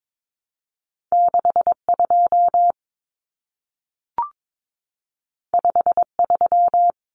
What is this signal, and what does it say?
Morse code, heard as a keyed tone, about 22 wpm, 700 Hz. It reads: 62 53
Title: Common Cut Numbers - 2-Digit 22wpm